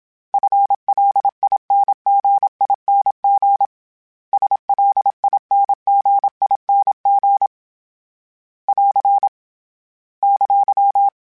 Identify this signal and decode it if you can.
Morse code, heard as a keyed tone, about 27 wpm, 800 Hz. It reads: FLINGING SLINGING <AR> !
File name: Special INGs / Special INGs-mp3s / Dble ings @25